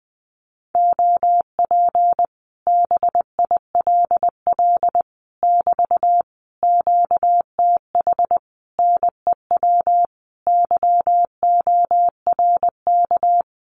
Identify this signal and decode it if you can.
Morse code, heard as a keyed tone, about 20 wpm, 700 Hz. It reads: OP BILL <BT> QTH NEW YORK
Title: Common QSO Phrases Encore 20wpm